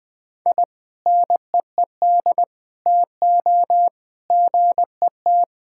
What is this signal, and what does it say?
Morse code, heard as a keyed tone, about 21 wpm, 700 Hz. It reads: I NEED TO GET
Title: Sentences with 2-Word Phrases - Spoken Beginning - 20wpm